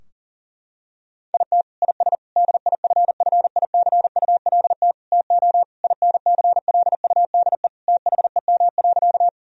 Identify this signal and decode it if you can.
Morse code, heard as a keyed tone, about 40 wpm, 700 Hz. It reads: IT IS DIFFICULT TO INCLUDE THEM.
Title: Sentences from Top 1000 Words - Review Part 3 - 40wpm